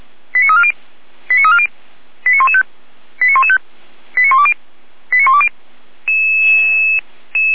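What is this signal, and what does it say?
Five-tone selective call (ZVEI), heard as quick runs of five tones, tones of about 70 ms. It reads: 89340 89340 89115 89115 89120 89120
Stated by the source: ZVEI five-tone selective call received by SDR, feuerwehr-orlamuende.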